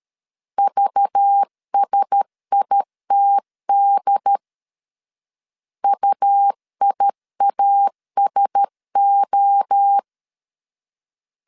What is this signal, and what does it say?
Morse code, heard as a keyed tone, about 13 wpm, 800 Hz. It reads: VSITD UIASO
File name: CW / DET / 1 6 1500C S C 12wpm 3533